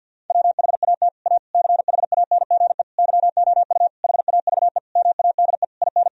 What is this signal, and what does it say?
Morse code, heard as a keyed tone, about 46 wpm, 700 Hz. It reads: WHAT A CHANGE YOU HAVE MADE IN
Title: Sentences from Top 1000 Words - Review Part 1 - 50wpm